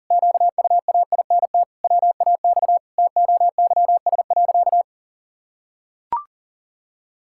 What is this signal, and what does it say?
Morse code, heard as a keyed tone, about 37 wpm, 700 Hz. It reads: QUAINT WAX TOYS.
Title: Pangram Challenge v2 - 40wpm